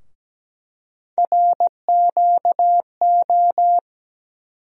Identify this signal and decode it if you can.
Morse code, heard as a keyed tone, about 17 wpm, 700 Hz. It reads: RQO
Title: USA Call Sign Postfixes - 3 Letters 17wpm